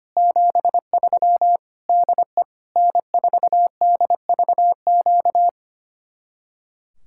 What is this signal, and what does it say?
Morse code, heard as a keyed tone, about 30 wpm, 700 Hz. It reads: 73 DE N4DVQ
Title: Speed-Racing Common QSO Phrases - 25wpm